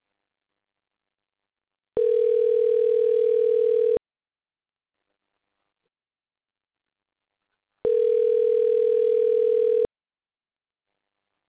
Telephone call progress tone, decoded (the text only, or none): ringback tone